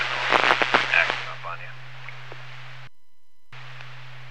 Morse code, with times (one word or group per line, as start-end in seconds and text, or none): none